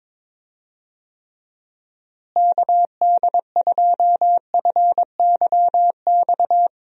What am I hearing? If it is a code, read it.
Morse code, heard as a keyed tone, about 22 wpm, 700 Hz. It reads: KD2FYX
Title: USA Call Signs - 6 Characters - Repeated 2x - 22wpm